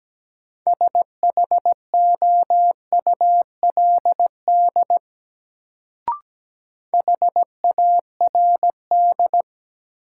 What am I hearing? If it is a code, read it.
Morse code, heard as a keyed tone, about 17 wpm, 700 Hz. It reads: SHOULD HARD
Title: Sentences from Top 300 Words - New Word Drill - 17wpm